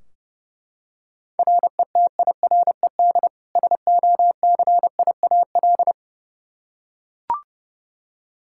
Morse code, heard as a keyed tone, about 30 wpm, 700 Hz. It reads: RETIRED SOCIAL
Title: Practice QSO - Exchange 3 - More Info - 30wpm